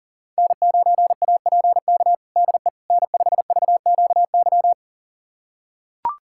Morse code, heard as a keyed tone, about 40 wpm, 700 Hz. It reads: N9APK DE N5VQY
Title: USA Callsign DE Callsign - 5 Characters - 40wpm